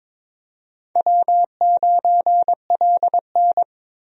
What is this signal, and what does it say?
Morse code, heard as a keyed tone, about 22 wpm, 700 Hz. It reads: W9LN
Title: USA Call Signs - 4 Characters - Repeated 3x - 22wpm